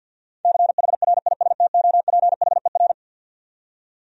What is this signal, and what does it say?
Morse code, heard as a keyed tone, about 52 wpm, 700 Hz. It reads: CHRISTOPHER
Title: USA Common Names - 50wpm